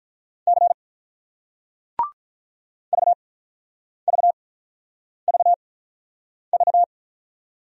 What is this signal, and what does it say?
Morse code, heard as a keyed tone, about 47 wpm, spaced as for 9 wpm, 700 Hz. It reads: C VVVV